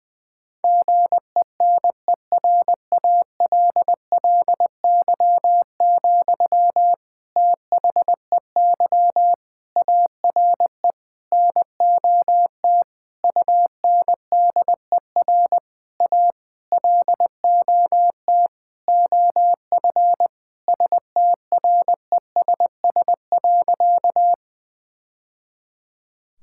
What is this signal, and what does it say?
Morse code, heard as a keyed tone, about 20 wpm, 700 Hz. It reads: GENERALLY, THEY ARE NOT UNDER A LOT OF STRESS.
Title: Sentences from Top 1000 Words - Review Part 6 - 20wpm